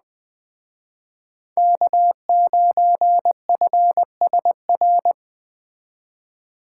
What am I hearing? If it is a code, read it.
Morse code, heard as a keyed tone, about 21 wpm, 700 Hz. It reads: K9FSR